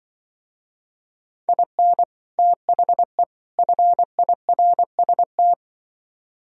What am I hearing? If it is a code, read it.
Morse code, heard as a keyed tone, about 25 wpm, 700 Hz. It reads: IN THE FIRST
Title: Speed-Racing - Sets of 3 Words 20wpm